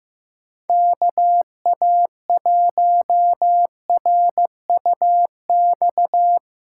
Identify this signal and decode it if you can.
Morse code, heard as a keyed tone, about 16 wpm, 700 Hz. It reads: KA1RUX
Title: USA Call Signs - 6 Characters - 2x Character Spacing - 15wpm